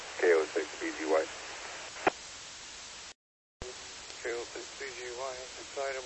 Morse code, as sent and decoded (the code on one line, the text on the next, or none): none